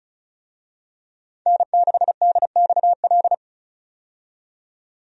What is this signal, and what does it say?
Morse code, heard as a keyed tone, about 34 wpm, 700 Hz. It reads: N6DXL
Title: USA Call Signs - 5 Characters - 35wpm